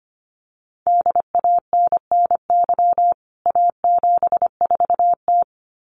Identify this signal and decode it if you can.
Morse code, heard as a keyed tone, about 25 wpm, 700 Hz. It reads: DANNY A74T
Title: CWT Exchange 25wpm